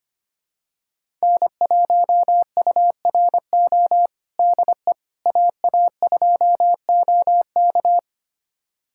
Morse code, heard as a keyed tone, about 25 wpm, 700 Hz. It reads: N1URO DE AA2OK
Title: USA Callsign DE Callsign - 5 Characters - 25wpm